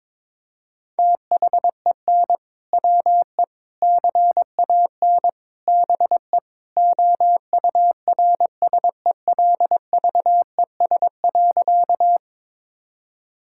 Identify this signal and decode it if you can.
Morse code, heard as a keyed tone, about 21 wpm, 700 Hz. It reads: THEN WE CAN BE OURSELVES.
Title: Sentences from Top 1600 Words 22wpm